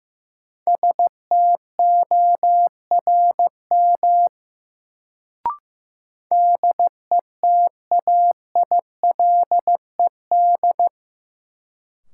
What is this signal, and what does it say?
Morse code, heard as a keyed tone, about 15 wpm, 700 Hz. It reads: STORM DETAILED
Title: Sentences from Top 1500 Words - New Word Drill Alt - 15wpm